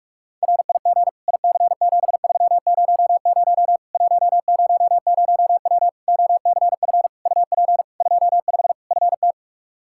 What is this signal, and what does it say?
Morse code, heard as a keyed tone, about 46 wpm, 700 Hz. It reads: RIG IC7300 100W OCF UP 25 FT
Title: Practice QSO - Exchange 2 - Info - 45wpm